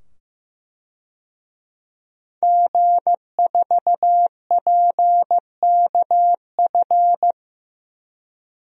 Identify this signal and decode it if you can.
Morse code, heard as a keyed tone, about 15 wpm, 700 Hz. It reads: G4PKF
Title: DXCC-100 Call Signs - 5 Characters - 15wpm